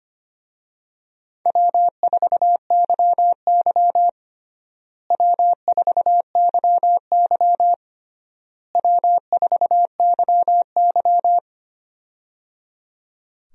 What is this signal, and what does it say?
Morse code, heard as a keyed tone, about 25 wpm, 700 Hz. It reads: W4YY W4YY W4YY
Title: USA Call Signs - 4 Characters - Repeated 3x - 25wpm